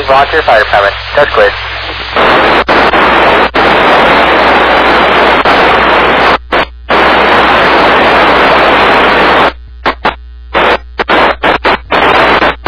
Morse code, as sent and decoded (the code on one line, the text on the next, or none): none